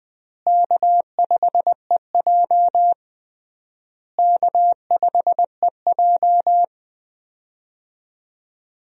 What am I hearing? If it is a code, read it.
Morse code, heard as a keyed tone, about 20 wpm, 690 Hz. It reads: K5EJ K5EJ
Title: USA Call Signs - 4 Characters - Repeated 3x - 20wpm